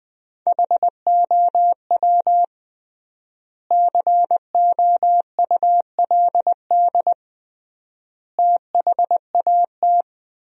Morse code, heard as a keyed tone, about 20 wpm, 700 Hz. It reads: HOW COULD THAT
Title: 3x Word-Spacing - Sentences from Top 400 Words - 20wpm